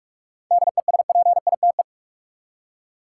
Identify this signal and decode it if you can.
Morse code, heard as a keyed tone, about 46 wpm, 700 Hz. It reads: DESPITE
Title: Seven Letter Words - 45wpm